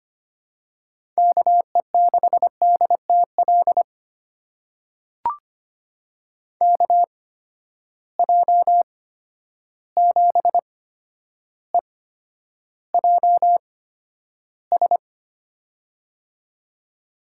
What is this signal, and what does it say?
Morse code, heard as a keyed tone, about 25 wpm, spaced as for 11 wpm, 700 Hz. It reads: KE6DTL KJ7EJS
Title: USA Call Signs - 8x Character Spacing - 25wpm